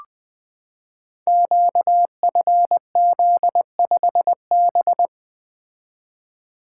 Morse code, heard as a keyed tone, about 20 wpm, 700 Hz. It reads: QFZ5B
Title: Code Groups - 5 Letter:Numbers - 20wpm